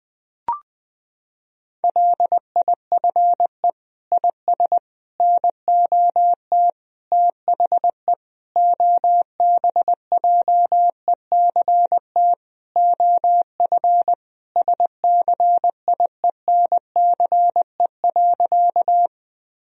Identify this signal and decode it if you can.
Morse code, heard as a keyed tone, about 20 wpm, 700 Hz. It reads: LIFE IS NOT THE OBJECT OF SCIENCE.
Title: Sentences from Top 1000 Words - Review Part 3 - 20wpm